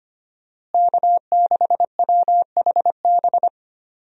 Morse code, heard as a keyed tone, about 25 wpm, 700 Hz. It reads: K6WHB
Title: USA Call Signs - 5 Characters - Repeated 2x - 25wpm